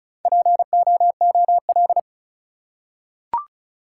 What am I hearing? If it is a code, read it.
Morse code, heard as a keyed tone, about 36 wpm, 700 Hz. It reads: POOL
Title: Four Letter Words - 35wpm